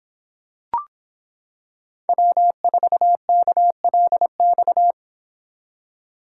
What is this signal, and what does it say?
Morse code, heard as a keyed tone, about 26 wpm, 700 Hz. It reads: W4KLX